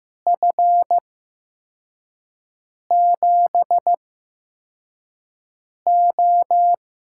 Morse code, heard as a keyed tone, about 15 wpm, spaced as for 4 wpm, 700 Hz. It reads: F7O
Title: USA Call Signs - 6 Characters - 8x Character Spacing - 15wpm